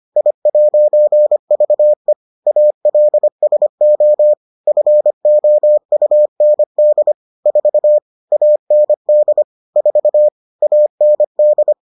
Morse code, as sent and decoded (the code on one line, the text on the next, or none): .. .----. ...- . / .- .-.. ... --- / ..-. --- ..- -. -.. / ....- / .- -. -.. / ....- / .- -. -..
I'VE ALSO FOUND 4 AND 4 AND